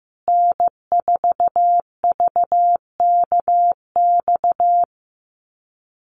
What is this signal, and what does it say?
Morse code, heard as a keyed tone, about 15 wpm, 700 Hz. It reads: N4VKX
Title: USA Call Signs Repeated 3x 15wpm